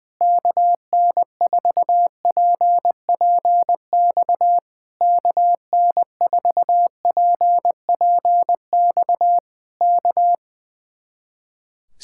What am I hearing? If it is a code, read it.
Morse code, heard as a keyed tone, about 19 wpm, 700 Hz. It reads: KN4PPX KN4PPX K
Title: Speed-Racing Common QSO Phrases Encore - 20wpm